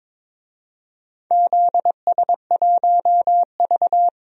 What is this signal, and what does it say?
Morse code, heard as a keyed tone, about 22 wpm, 700 Hz. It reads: ZS1V
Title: DXCC-100 Call Signs - 4 Characters - 22wpm